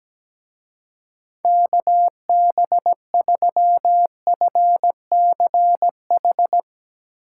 Morse code, heard as a keyed tone, about 17 wpm, 700 Hz. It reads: KB3FCH
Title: Speed Racing - USA 2x3 Call Signs 17wpm